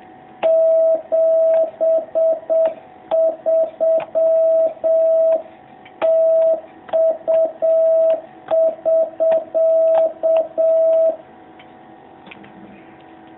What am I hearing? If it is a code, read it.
Morse code, heard as a keyed tone, about 7 wpm, 660 Hz. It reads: 73 TU <SK>
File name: jh8cbh / 2019cw